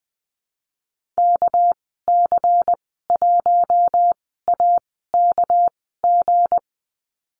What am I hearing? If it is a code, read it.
Morse code, heard as a keyed tone, about 20 wpm, spaced as for 15 wpm, 700 Hz. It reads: KC1AKG
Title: 200 Lesson 32g - Letter k - Call Signs 20wpm